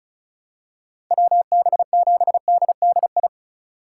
Morse code, heard as a keyed tone, about 36 wpm, 700 Hz. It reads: WB7DDI
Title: USA Call Signs - 6 Characters - 3x Character Spacing - 35wpm